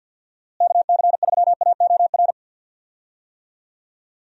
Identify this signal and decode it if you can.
Morse code, heard as a keyed tone, about 46 wpm, 700 Hz. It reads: KC3AOR
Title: USA Call Signs - 6 Characters - Repeated 2x - 50wpm